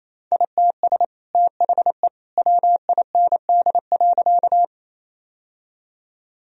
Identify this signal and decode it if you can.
Morse code, heard as a keyed tone, about 28 wpm, 700 Hz. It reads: ITS THE WIND.
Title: Sentences from Top 1000 Words - Review Part 2 - 28wpm